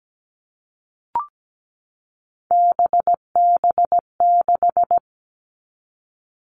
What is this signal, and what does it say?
Morse code, heard as a keyed tone, about 17 wpm, 700 Hz. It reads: BB6
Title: All Cut Numbers - 3-Digit 17wpm